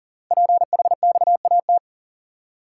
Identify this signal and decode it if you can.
Morse code, heard as a keyed tone, about 40 wpm, 700 Hz. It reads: PHXAT
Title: Code Groups - 5 Letter:Numbers - 40wpm